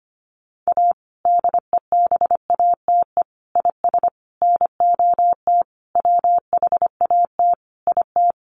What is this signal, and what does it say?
Morse code, heard as a keyed tone, about 25 wpm, 700 Hz. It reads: A DEBATE IS NOT WHAT IT
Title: Sentences from Top 1100 Words 25wpm